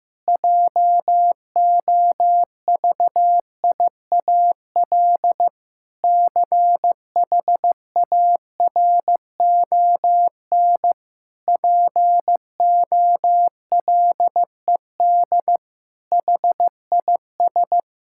Morse code, heard as a keyed tone, about 16 wpm, 700 Hz. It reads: JOVIAL CHARON POLED HIS
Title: Pangram Challenge v2 - 15wpm